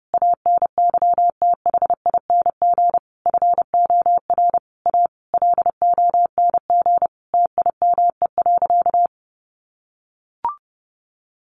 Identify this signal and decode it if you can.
Morse code, heard as a keyed tone, about 30 wpm, 700 Hz. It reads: ANYTHING FOR A LONG TIME.
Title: Sentences from Top 1000 Words 30wpm